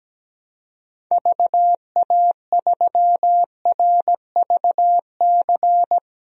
Morse code, heard as a keyed tone, about 17 wpm, 700 Hz. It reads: VA3RVC
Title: DXCC-100 Call Signs - 6 Characters - 17wpm